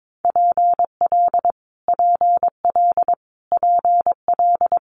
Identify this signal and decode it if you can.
Morse code, heard as a keyed tone, about 22 wpm, 700 Hz. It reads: PL PL PL
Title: Multi-letter Phonemes Repeated 3x 22wpm